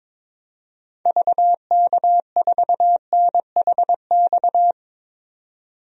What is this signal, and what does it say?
Morse code, heard as a keyed tone, about 22 wpm, 700 Hz. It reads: VK4NHX
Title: DXCC-100 Call Signs - 6 Characters - 22wpm